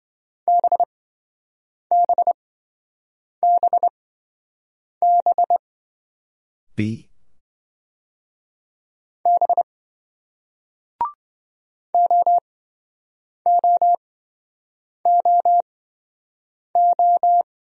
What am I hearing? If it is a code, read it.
Morse code, heard as a keyed tone, about 27 wpm, spaced as for 7 wpm, 700 Hz. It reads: BBBB B OOOO